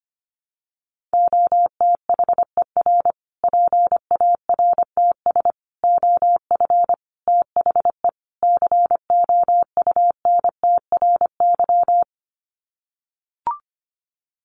Morse code, heard as a keyed tone, about 25 wpm, 700 Hz. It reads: OTHER PARTS OF THE COUNTRY
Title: Sets of 5 Words Encore 25wpm